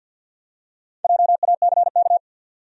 Morse code, heard as a keyed tone, about 50 wpm, 700 Hz. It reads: JACK
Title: USA Common Names - Max 4 Letters - 50wpm